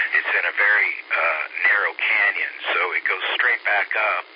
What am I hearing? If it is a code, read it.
Emergency Alert System (SAME) tones, heard as harsh buzzing bursts, with speech. It is heard from 0.4 to 3.9 s.